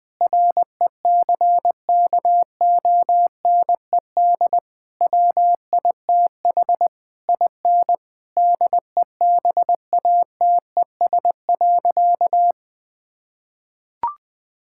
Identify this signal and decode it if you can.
Morse code, heard as a keyed tone, about 20 wpm, 700 Hz. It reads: RECKONED WITH IN DEBATES.
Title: Sentences with 6-Word Phrases - 20wpm